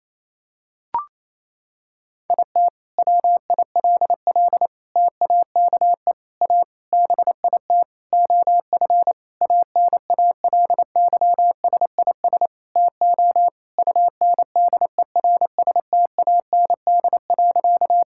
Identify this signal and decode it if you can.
Morse code, heard as a keyed tone, about 28 wpm, 700 Hz. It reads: IT WILL TAKE A BIT OF ANALYSIS TO UNDERSTAND.
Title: Sentences from Top 1000 Words - Review Part 5 - 28wpm